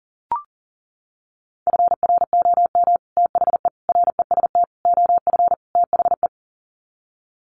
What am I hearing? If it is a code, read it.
Morse code, heard as a keyed tone, about 40 wpm, 700 Hz. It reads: FROM THE REST OF THE
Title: Sets of 5 Words 40wpm